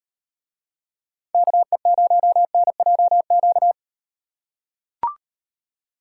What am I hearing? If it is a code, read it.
Morse code, heard as a keyed tone, about 38 wpm, 700 Hz. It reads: KE0NJQ